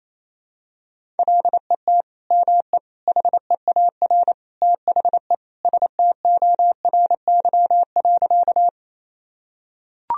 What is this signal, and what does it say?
Morse code, heard as a keyed tone, about 28 wpm, 700 Hz. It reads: LET ME HEAR THE STORY.
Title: Sentences from Top 1000 Words - Review Part 1 - 28wpm